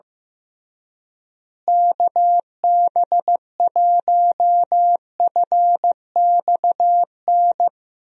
Morse code, heard as a keyed tone, about 15 wpm, 700 Hz. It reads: KB1FXN
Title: USA Call Signs - 6 Characters - Repeated 3x - 15wpm